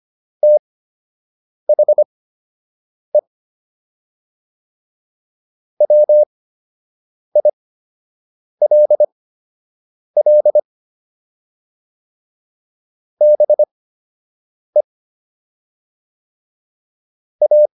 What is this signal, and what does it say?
Morse code, heard as a keyed tone, about 25 wpm, spaced as for 7 wpm, 600 Hz. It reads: THE WILL BE A